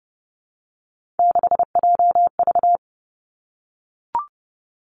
Morse code, heard as a keyed tone, about 30 wpm, 700 Hz. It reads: 6JV
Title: Three Challenging Letter-Number Combinations 30wpm